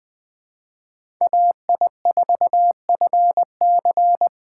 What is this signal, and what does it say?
Morse code, heard as a keyed tone, about 20 wpm, 700 Hz. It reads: AI4FC
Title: Speed Racing - USA 2x2 Call Signs 17wpm